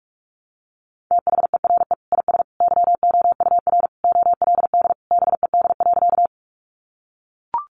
Morse code, heard as a keyed tone, about 44 wpm, 700 Hz. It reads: THERE IS YOUR OLD BED.
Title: Sentences from Top 500 Words 45wpm